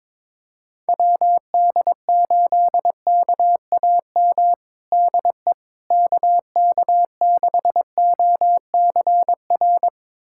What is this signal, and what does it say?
Morse code, heard as a keyed tone, about 22 wpm, 700 Hz. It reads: WD8KAM DE KK6OCR
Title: USA Callsign DE Callsign 22wpm